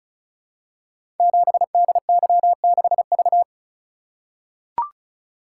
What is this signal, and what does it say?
Morse code, heard as a keyed tone, about 36 wpm, 690 Hz. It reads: 7DY6V
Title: Code Groups - 5 Letter:Numbers - 35wpm